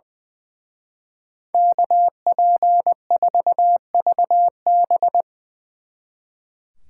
Morse code, heard as a keyed tone, about 22 wpm, 700 Hz. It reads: KP4VB
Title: Speed Racing - USA Call Signs 20wpm